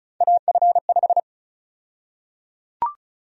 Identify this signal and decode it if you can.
Morse code, heard as a keyed tone, about 34 wpm, 700 Hz. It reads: AF5
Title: USA Call Sign Prefixes - 35wpm